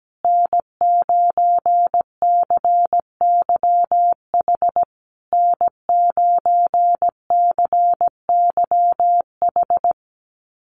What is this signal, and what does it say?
Morse code, heard as a keyed tone, about 17 wpm, 700 Hz. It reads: N9CYH N9CYH
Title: USA Call Signs Repeated 3x 17wpm